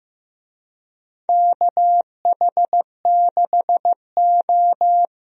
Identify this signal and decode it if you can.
Morse code, heard as a keyed tone, about 15 wpm, 700 Hz. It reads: KH6O
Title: DXCC-100 Call Signs - 4 Characters - 15wpm